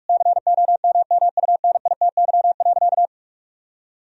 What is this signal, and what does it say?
Morse code, heard as a keyed tone, about 46 wpm, 700 Hz. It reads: COMMUNITY.
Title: Sentences from Top 1000 Words - Review Part 4 - 45wpm